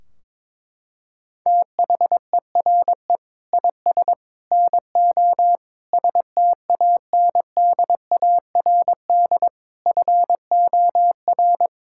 Morse code, heard as a keyed tone, about 22 wpm, 700 Hz. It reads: THERE IS NO STANDARD FOR
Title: Sentences from Top 1600 Words 22wpm